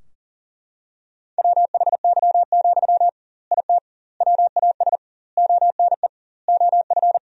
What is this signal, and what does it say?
Morse code, heard as a keyed tone, about 37 wpm, 700 Hz. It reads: WHY, IT WAS ONE OF
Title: 2x Word-Spacing - Sentences from Top 300 Words - 40wpm